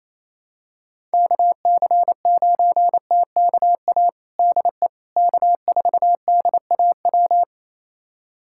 Morse code, heard as a keyed tone, about 28 wpm, 700 Hz. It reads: KC9TKA DE K4DAW
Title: USA Callsign DE Callsign 28wpm